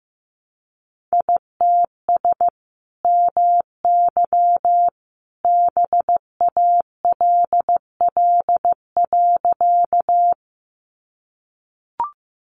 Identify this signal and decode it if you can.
Morse code, heard as a keyed tone, about 15 wpm, 700 Hz. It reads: ITS MY BALL.
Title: Sentences from Top 900 Words 15wpm